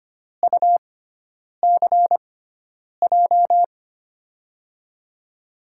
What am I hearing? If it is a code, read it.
Morse code, heard as a keyed tone, about 25 wpm, spaced as for 9 wpm, 700 Hz. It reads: UCJ
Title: USA Call Signs - 5 Characters - 6x Character Spacing - 25wpm